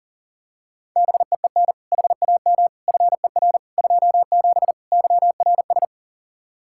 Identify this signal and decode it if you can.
Morse code, heard as a keyed tone, about 40 wpm, 700 Hz. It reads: BEEN HAM FER 27 YRS
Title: Practice QSO - Exchange 3 - More Info - 40wpm